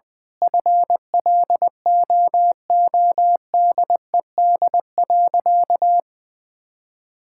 Morse code, heard as a keyed tone, about 20 wpm, 700 Hz. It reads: FLOODED.
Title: Sentences with 6-Word Phrases - 20wpm